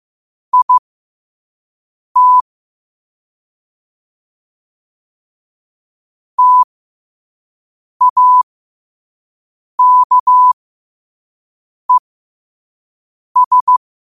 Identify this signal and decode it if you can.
Morse code, heard as a keyed tone, about 15 wpm, spaced as for 5 wpm, 1000 Hz. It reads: IT TAKES
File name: qotdfiles / 5wpm